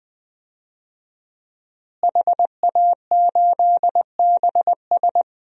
Morse code, heard as a keyed tone, about 20 wpm, 700 Hz. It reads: HA8BS
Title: DXCC-100 Call Signs - 5 Characters - Repeated 2x - 20wpm